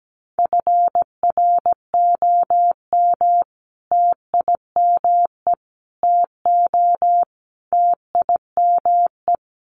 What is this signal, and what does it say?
Morse code, heard as a keyed tone, about 17 wpm, 700 Hz. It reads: FROM TIME TO TIME
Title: Sets of 4 Words 17wpm